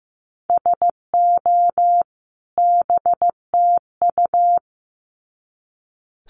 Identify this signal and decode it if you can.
Morse code, heard as a keyed tone, about 15 wpm, 700 Hz. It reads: SO BTU
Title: Common QSO Phrases 15wpm